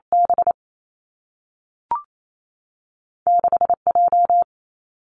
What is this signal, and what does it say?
Morse code, heard as a keyed tone, about 28 wpm, 700 Hz. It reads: B 6 J
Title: Two Challenging Letter-Number Combinations 28wpm